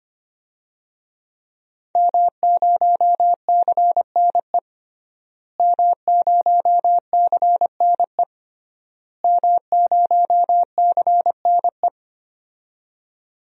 Morse code, heard as a keyed tone, about 25 wpm, 700 Hz. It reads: M0CNE M0CNE M0CNE
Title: DXCC-100 Call Signs - 5 Characters - Repeated 3x - 25wpm